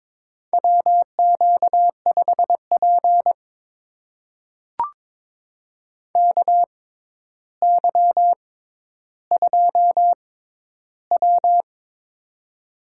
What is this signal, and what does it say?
Morse code, heard as a keyed tone, about 22 wpm, spaced as for 11 wpm, 700 Hz. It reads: WQ5P KY2W
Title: USA Call Signs - 6x Character Spacing - 22wpm